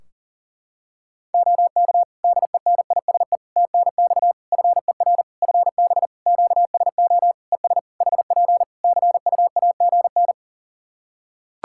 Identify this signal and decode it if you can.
Morse code, heard as a keyed tone, about 40 wpm, 700 Hz. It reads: OK DENISE TNX FER FB QSO ES HP CUAGN
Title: Practice QSO - Exchange 4 - Ending - 40wpm